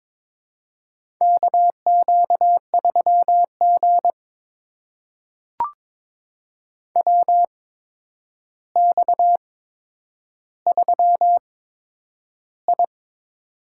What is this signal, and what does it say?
Morse code, heard as a keyed tone, about 22 wpm, spaced as for 10 wpm, 700 Hz. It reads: KQ3G WX3I
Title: USA Call Signs - 4 Characters - 8x Character Spacing - 22wpm